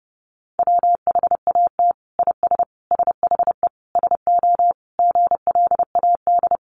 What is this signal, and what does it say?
Morse code, heard as a keyed tone, about 30 wpm, 700 Hz. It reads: WHAT IS SHE SO GLAD
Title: Sentences from Top 800 Words 30wpm